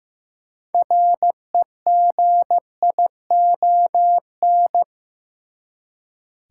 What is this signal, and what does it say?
Morse code, heard as a keyed tone, about 16 wpm, 700 Hz. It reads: REGION
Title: Six Letter Words - 15wpm